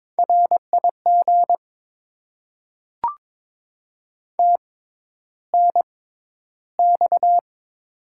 Morse code, heard as a keyed tone, about 22 wpm, spaced as for 11 wpm, 700 Hz. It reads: RIG TNX
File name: Top 20 QSO Elements - 6x Character Spacing - 22wpm